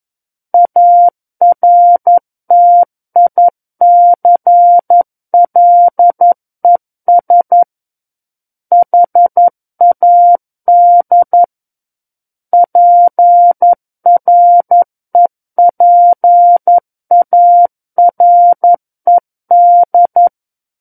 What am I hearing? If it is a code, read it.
Morse code, heard as a keyed tone, about 11 wpm, 700 Hz. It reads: ARTICLES HAD PREPARED
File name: Book One / 02-Chapter 2 - 11 WPM